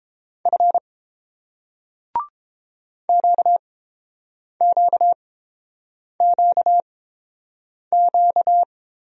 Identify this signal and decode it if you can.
Morse code, heard as a keyed tone, about 30 wpm, spaced as for 7 wpm, 700 Hz. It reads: F QQQQ